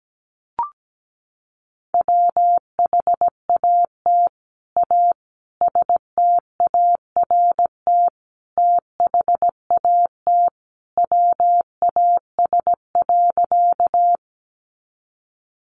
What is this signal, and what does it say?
Morse code, heard as a keyed tone, about 17 wpm, 700 Hz. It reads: WHAT A START THAT WAS.
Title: Sentences from Top 300 Words 17wpm